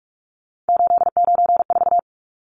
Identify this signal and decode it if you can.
Morse code, heard as a keyed tone, about 44 wpm, 700 Hz. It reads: 894
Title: Numbers - 3-Digit 45wpm